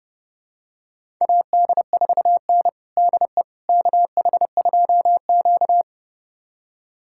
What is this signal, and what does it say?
Morse code, heard as a keyed tone, about 30 wpm, 700 Hz. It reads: AD4N DE KH2Q
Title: USA Callsign DE Callsign - 4 Characters - 30wpm